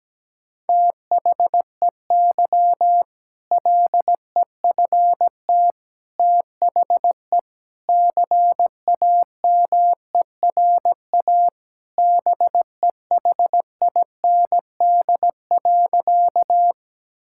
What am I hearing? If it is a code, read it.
Morse code, heard as a keyed tone, about 17 wpm, 700 Hz. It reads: THEY LEFT THE CAMERA BEHIND.
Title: Sentences from Top 1000 Words - Review Part 5 - 17wpm